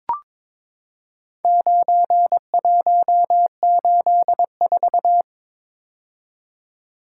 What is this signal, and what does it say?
Morse code, heard as a keyed tone, about 22 wpm, 700 Hz. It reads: 9184
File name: practice / 22wpm / Four Letter-Number Combinations - 22wpm